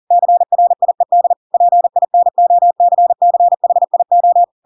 Morse code, heard as a keyed tone, about 40 wpm, 700 Hz. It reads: CRIED PINOCCHIO
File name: Pinocchio / Pinocchio ch7 40 40 ik1hge